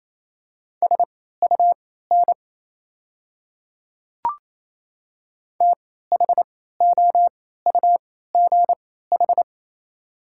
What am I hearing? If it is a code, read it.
Morse code, heard as a keyed tone, about 28 wpm, spaced as for 16 wpm, 690 Hz. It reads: SUN THOUGH
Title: 3x Character Spacing - Top 500 Words - 28wpm